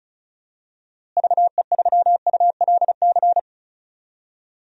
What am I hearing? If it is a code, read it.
Morse code, heard as a keyed tone, about 36 wpm, 700 Hz. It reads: VE3ULC
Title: DXCC-100 Call Signs - 6 Characters - Repeated 2x - 35wpm